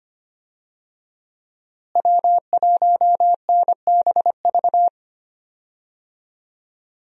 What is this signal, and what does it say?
Morse code, heard as a keyed tone, about 25 wpm, 700 Hz. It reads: W1NBV
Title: USA Call Signs - 5 Characters - Repeated 2x - 25wpm